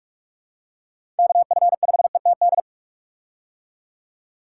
Morse code, heard as a keyed tone, about 46 wpm, 700 Hz. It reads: KF5ETD
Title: K1USN SST QSO Practice - 45wpm